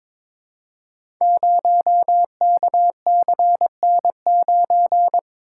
Morse code, heard as a keyed tone, about 23 wpm, 700 Hz. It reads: 0KCN9
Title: Code Groups - 5 Letter:Numbers - 22wpm